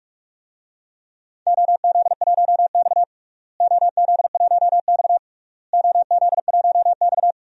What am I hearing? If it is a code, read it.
Morse code, heard as a keyed tone, about 46 wpm, 700 Hz. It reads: OZ1X OZ1X OZ1X
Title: DXCC-100 Call Signs - 4 Characters - Repeated 3x - 45wpm